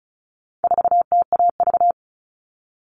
Morse code, heard as a keyed tone, about 34 wpm, 700 Hz. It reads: 4TAV
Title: All Cut Numbers - 4-Digit 35wpm